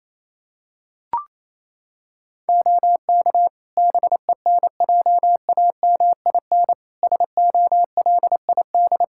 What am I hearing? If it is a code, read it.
Morse code, heard as a keyed tone, about 28 wpm, 700 Hz. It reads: OK BENJAMIN SOLID
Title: Practice QSO - Exchange 3 - More Info - 28wpm